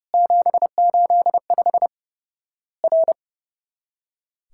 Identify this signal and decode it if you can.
Morse code, heard as a keyed tone, about 30 wpm, 700 Hz. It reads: 785 R
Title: Worked All Europe DX Contest - QTC Exchange - 30wpm